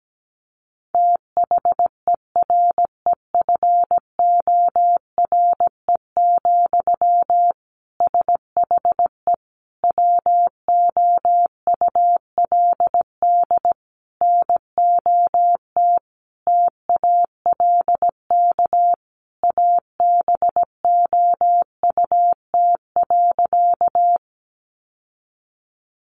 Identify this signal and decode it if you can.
Morse code, heard as a keyed tone, about 17 wpm, 700 Hz. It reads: THEREFORE, SHE WOULD NOT TALK ABOUT.
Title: Sentences from Top 800 Words 17wpm